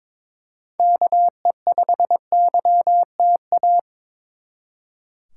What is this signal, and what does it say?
Morse code, heard as a keyed tone, about 26 wpm, 700 Hz. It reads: KE5YTA
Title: Speed Racing - USA 2x3 Call Signs 22wpm